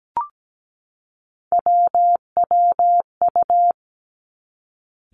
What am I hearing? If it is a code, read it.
Morse code, heard as a keyed tone, about 17 wpm, 700 Hz. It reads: WWU
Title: Three Challenging Letter-Number Combinations 17wpm